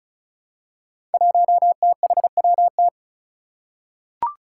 Morse code, heard as a keyed tone, about 34 wpm, 700 Hz. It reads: 1THWT
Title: Five Challenging Letter-Number Combinations 35wpm